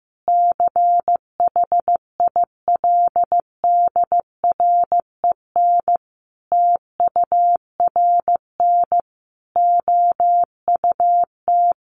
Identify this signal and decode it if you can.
Morse code, heard as a keyed tone, about 15 wpm, 700 Hz. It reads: CHILDREN TURN OUT
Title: Sentences from Top 600 Words 15wpm